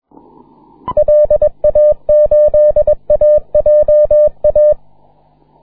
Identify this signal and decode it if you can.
Morse code, heard as a keyed tone, about 22 wpm, 560 Hz. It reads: LA8AJA
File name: wp / cx2aq 12m cw 21 mar 2010 1419z